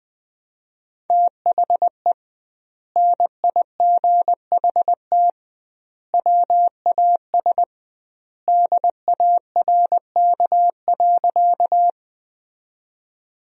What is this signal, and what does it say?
Morse code, heard as a keyed tone, about 21 wpm, 700 Hz. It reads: THE NIGHT WAS DARK.
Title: 2x Word-Spacing - Sentences from Top 500 Words - 20wpm